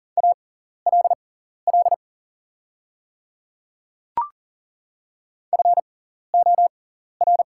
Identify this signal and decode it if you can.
Morse code, heard as a keyed tone, about 40 wpm, spaced as for 14 wpm, 700 Hz. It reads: ALL FOR
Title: 6x Character Spacing - Top 100 Words - 40wpm